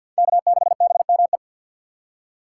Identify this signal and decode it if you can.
Morse code, heard as a keyed tone, about 52 wpm, 690 Hz. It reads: K6BME
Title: USA Call Signs - 5 Characters - 2x Character Spacing - 50wpm